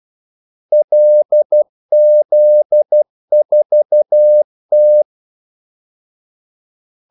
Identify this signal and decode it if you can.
Morse code, heard as a keyed tone, about 12 wpm, 600 Hz. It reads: LZ4T
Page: 2 ekstra ordmellemrum mellem kaldesignalerne